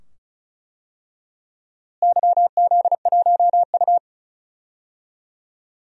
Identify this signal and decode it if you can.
Morse code, heard as a keyed tone, about 36 wpm, 700 Hz. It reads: YZ1U
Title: DXCC-100 Call Signs - 4 Characters - Repeated 2x - 35wpm